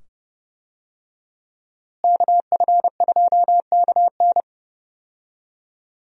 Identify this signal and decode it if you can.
Morse code, heard as a keyed tone, about 30 wpm, 700 Hz. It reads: KF2KN
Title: USA Call Signs - 5 Characters - Repeated 2x - 30wpm